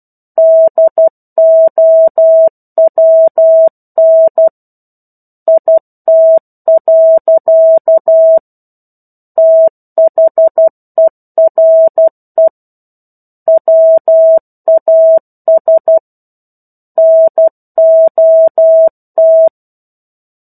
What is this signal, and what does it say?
Morse code, heard as a keyed tone, about 12 wpm, 650 Hz. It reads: DOWN IT. THERE WAS NOT
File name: 01-Chapter-01 12 WPM